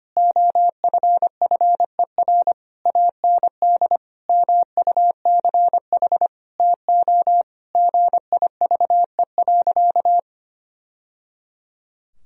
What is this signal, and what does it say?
Morse code, heard as a keyed tone, about 25 wpm, 700 Hz. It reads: OFFER AND MUCH TO GIVE.
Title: Sentences from Top 1000 Words - Review Part 4 - 25wpm